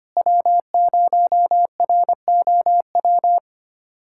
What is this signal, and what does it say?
Morse code, heard as a keyed tone, about 25 wpm, 700 Hz. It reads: W0ROW
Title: USA Callsign DE Callsign - 5 Characters - 25wpm